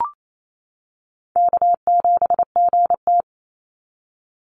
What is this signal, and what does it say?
Morse code, heard as a keyed tone, about 28 wpm, 700 Hz. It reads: K7GT
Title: NA SOTA Chaser Callsigns 28wpm